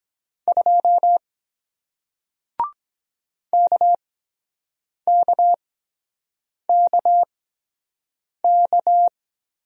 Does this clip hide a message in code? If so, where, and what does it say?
Yes: Morse code from 0.5 to 9.1 s, heard as a keyed tone, about 23 wpm, spaced as for 7 wpm, 700 Hz. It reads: 2 KKKK